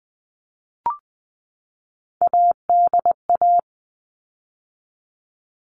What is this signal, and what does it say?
Morse code, heard as a keyed tone, about 20 wpm, 700 Hz. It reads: ADA
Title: All Cut Numbers - 3-Digit 20wpm